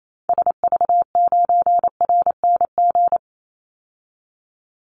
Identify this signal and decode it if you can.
Morse code, heard as a keyed tone, about 28 wpm, 700 Hz. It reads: SV9RNG
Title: Call Signs Repeated 3x - 28wpm